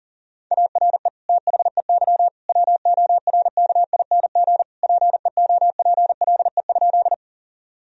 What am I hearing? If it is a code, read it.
Morse code, heard as a keyed tone, about 40 wpm, 700 Hz. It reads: ARE THEY WORKING PEOPLE?
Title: Sentences from Top 1000 Words - Review Part 4 - 40wpm